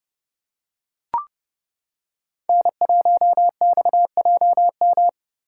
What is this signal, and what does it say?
Morse code, heard as a keyed tone, about 30 wpm, 700 Hz. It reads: N1XJM
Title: K1USN SST QSO Practice - 30wpm